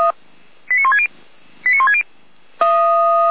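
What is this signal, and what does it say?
Five-tone selective call (ZVEI), heard as quick runs of five tones, tones of about 70 ms. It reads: 89160 89160
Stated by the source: Five-tone ZVEI selective call received on an SDR, fire brigade alerting.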